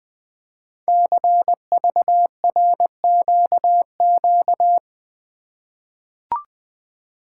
Morse code, heard as a keyed tone, about 20 wpm, 700 Hz. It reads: CVRQQ
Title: Code Groups - 5 Letters - 20wpm